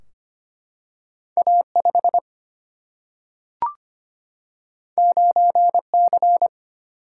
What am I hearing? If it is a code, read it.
Morse code, heard as a keyed tone, about 25 wpm, 700 Hz. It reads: A5 9C
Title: Single-Letter Single-Number 25wpm